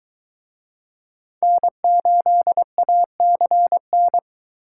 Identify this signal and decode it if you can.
Morse code, heard as a keyed tone, about 23 wpm, 700 Hz. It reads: N8ACN